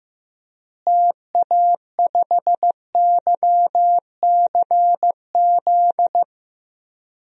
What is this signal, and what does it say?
Morse code, heard as a keyed tone, about 15 wpm, 700 Hz. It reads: TA5YCZ
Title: DXCC-100 Call Signs - 6 Characters - Repeated 3x - 15wpm